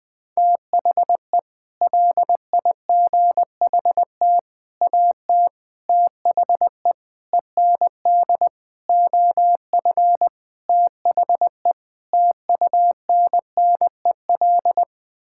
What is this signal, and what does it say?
Morse code, heard as a keyed tone, about 20 wpm, 700 Hz. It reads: THE LIGHT AT THE END OF THE TUNNEL
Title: Sets of 8 Words - 20wpm